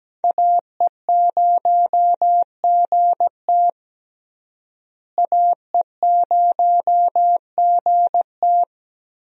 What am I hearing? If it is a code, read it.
Morse code, heard as a keyed tone, about 17 wpm, 700 Hz. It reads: AE0GT AE0GT
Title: USA Call Signs - 5 Characters - Repeated 2x - 17wpm